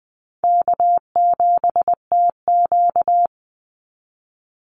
Morse code, heard as a keyed tone, about 20 wpm, 700 Hz. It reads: K7TQ
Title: NA SOTA Chaser Callsigns 20wpm